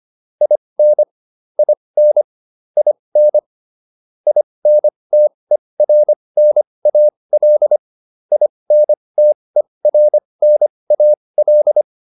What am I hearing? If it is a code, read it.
Morse code, heard as a keyed tone, about 25 wpm, spaced as for 20 wpm, 600 Hz. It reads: IN IN IN INTERNAL INTERNAL
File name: IN8 20